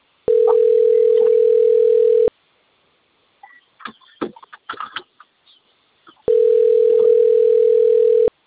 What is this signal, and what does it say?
Ringback tone (North America), 440 + 480 Hz.